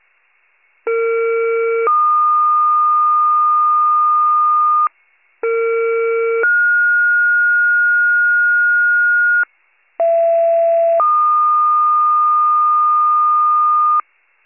Two-tone page, heard as a one-second tone, then a three-second tone, about 460 then 1220 Hz, about 460 then 1430 Hz, about 670 then 1220 Hz. It is heard from 0.9 to 14.0 s.